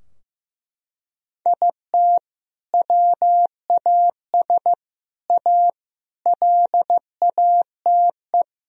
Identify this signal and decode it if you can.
Morse code, heard as a keyed tone, about 16 wpm, 700 Hz. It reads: IT WAS A LATE
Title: Sentences from Top 1000 Words - Review Part 4 - 15wpm